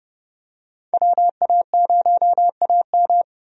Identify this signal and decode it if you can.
Morse code, heard as a keyed tone, about 30 wpm, 700 Hz. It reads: WA0AM
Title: USA Call Signs - 5 Characters - 30wpm